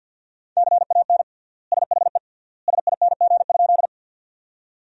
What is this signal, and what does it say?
Morse code, heard as a keyed tone, about 50 wpm, 700 Hz. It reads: CAN SHE SING?
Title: 3x Word-Spacing - Sentences from Top 400 Words - 50wpm